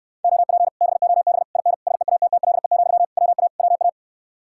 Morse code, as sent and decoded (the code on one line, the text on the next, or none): --- .--. / -... --- -... / .. -. / ... .- - - .-.. --..-- / .-- .- / .-- .-
OP BOB IN SEATTLE, WA WA